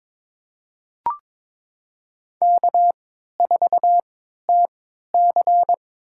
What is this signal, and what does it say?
Morse code, heard as a keyed tone, about 22 wpm, spaced as for 12 wpm, 700 Hz. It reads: K4TC
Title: USA Call Signs - 3x Character Spacing - 22wpm